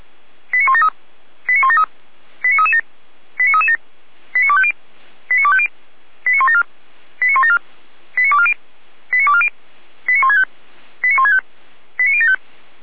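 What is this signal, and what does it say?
Five-tone selective call (ZVEI), heard as quick runs of five tones, tones of about 70 ms. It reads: 89173 89173 89337 89337 89250 89250 89174 89174 89240 89240 89165 89165 89075
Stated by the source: SDR-received ZVEI five-tone selective call, from the Feuerwehr Orlamünde audio archive.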